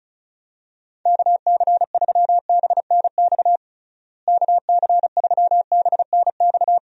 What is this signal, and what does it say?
Morse code, heard as a keyed tone, about 36 wpm, 700 Hz. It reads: KC3BNX KC3BNX
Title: USA Call Signs - 6 Characters - Repeated 2x - 35wpm